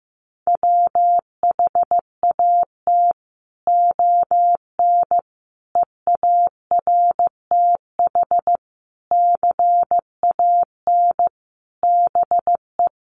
Morse code, heard as a keyed tone, about 14 wpm, 700 Hz. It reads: WHAT ON EARTH CAN BE
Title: Sentences from Top 700 Words 15wpm